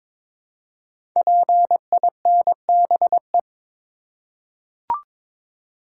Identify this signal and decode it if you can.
Morse code, heard as a keyed tone, about 22 wpm, 700 Hz. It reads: PINBE
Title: Code Groups - 5 Letter:Numbers - 22wpm